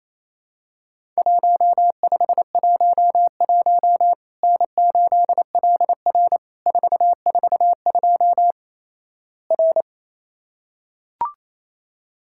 Morse code, heard as a keyed tone, about 27 wpm, 700 Hz. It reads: 1511 N8LR 442 R
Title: Worked All Europe DX Contest - QTC Exchange - 28wpm